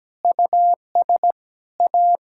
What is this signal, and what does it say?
Morse code, heard as a keyed tone, about 17 wpm, 700 Hz. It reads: US A
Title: Sentences from Top 1000 Words - Review Part 5 - 17wpm